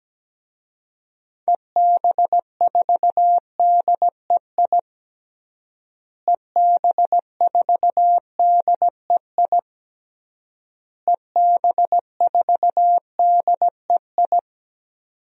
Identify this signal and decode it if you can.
Morse code, heard as a keyed tone, about 17 wpm, 700 Hz. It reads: EB4DEI EB4DEI EB4DEI
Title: DXCC-100 Call Signs - 6 Characters - Repeated 3x - 17wpm